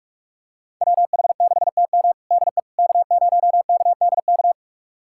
Morse code, heard as a keyed tone, about 42 wpm, 700 Hz. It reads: WH6TM DE K0KDK
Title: USA Callsign DE Callsign - 5 Characters - 45wpm